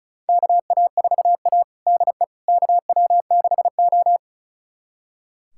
Morse code, heard as a keyed tone, about 36 wpm, 700 Hz. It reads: KA4A DE KW6O
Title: USA Callsign DE Callsign - 4 Characters - 35wpm